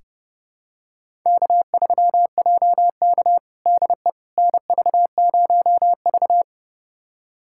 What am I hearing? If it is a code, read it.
Morse code, heard as a keyed tone, about 32 wpm, 700 Hz. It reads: K3JK DE NV0V
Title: USA Callsign DE Callsign - 4 Characters - 30wpm